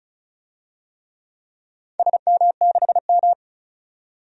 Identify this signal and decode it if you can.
Morse code, heard as a keyed tone, about 36 wpm, 700 Hz. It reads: SM6M
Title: DXCC-100 Call Signs - 4 Characters - 35wpm